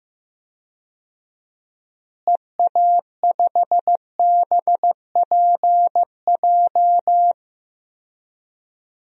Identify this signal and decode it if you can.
Morse code, heard as a keyed tone, about 15 wpm, 700 Hz. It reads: EA5BPJ
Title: DXCC-100 Call Signs - 6 Characters - Repeated 3x - 15wpm